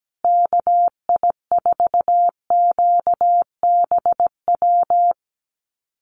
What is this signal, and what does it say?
Morse code, heard as a keyed tone, about 17 wpm, 700 Hz. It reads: KI4QBW
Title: USA Call Signs Repeated 2x 17wpm